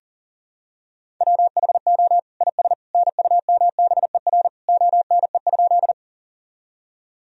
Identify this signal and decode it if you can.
Morse code, heard as a keyed tone, about 40 wpm, 700 Hz. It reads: WHO IS NUMBER ONE?
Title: Sentences from Top 1000 Words - Review Part 1 - 40wpm